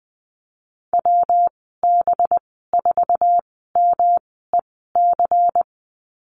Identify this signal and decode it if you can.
Morse code, heard as a keyed tone, about 20 wpm, spaced as for 15 wpm, 700 Hz. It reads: WB4MEC
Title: 193 Lesson 31g - Letter v - Call Signs 20wpm